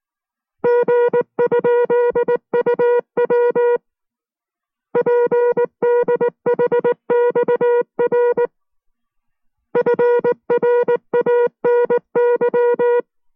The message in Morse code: G?UW PDHXR FRANY